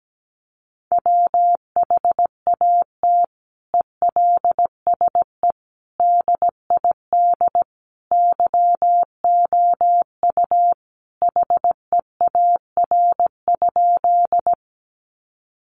Morse code, heard as a keyed tone, about 17 wpm, 700 Hz. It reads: WHAT ELSE DID YOU HEAR?
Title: Sentences from Top 800 Words 17wpm